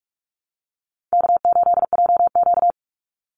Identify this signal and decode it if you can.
Morse code, heard as a keyed tone, about 44 wpm, 700 Hz. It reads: K8JQ
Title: Easy Call Signs - 45wpm